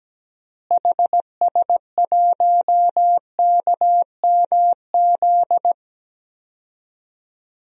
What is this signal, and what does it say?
Morse code, heard as a keyed tone, about 17 wpm, 700 Hz. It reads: HS1KMZ
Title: DXCC-100 Call Signs - 6 Characters - Repeated 2x - 17wpm